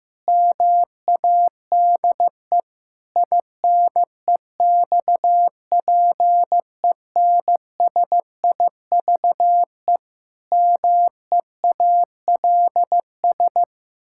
Morse code, heard as a keyed tone, about 14 wpm, 700 Hz. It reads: MADE INEXPENSIVE MEALS
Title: Pangram Challenge v2 - 15wpm